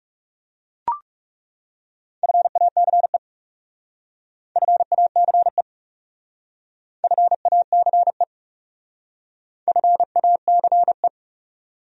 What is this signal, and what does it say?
Morse code, heard as a keyed tone, about 37 wpm, 700 Hz. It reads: FACE FACE FACE FACE
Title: Speed-Racing Top 500 Words 30wpm